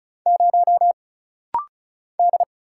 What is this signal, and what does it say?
Morse code, heard as a keyed tone, about 36 wpm, 700 Hz. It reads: 0 D
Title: Single Letter-Number - Rapid-Fire 35wpm